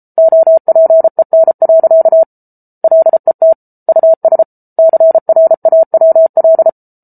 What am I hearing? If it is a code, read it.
Morse code, heard as a keyed tone, about 34 wpm, 650 Hz. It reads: OPEN. LET US CRAWL